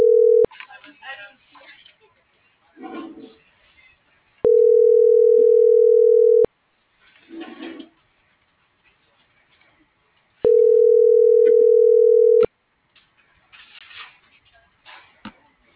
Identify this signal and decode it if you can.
Ringback tone (North America), 440 + 480 Hz.